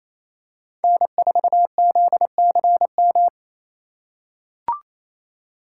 Morse code, heard as a keyed tone, about 28 wpm, 700 Hz. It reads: N4ZCM
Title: USA Call Signs - 4x Character Spacing - 28wpm